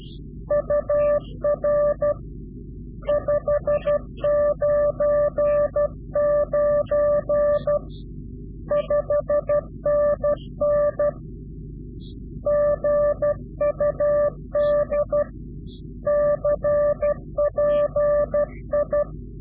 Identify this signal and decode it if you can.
Morse code, heard as a keyed tone, about 12 wpm, 590 Hz. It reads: UR 599 5NN GUD CPI
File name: CW-QSO / Directional Call / Directional QSO